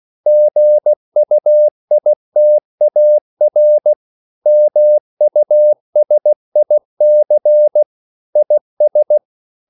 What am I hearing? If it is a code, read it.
Morse code, heard as a keyed tone, about 16 wpm, 600 Hz. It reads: GUITAR MUSIC IS
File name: CW / Practice / EnglishQuotations0416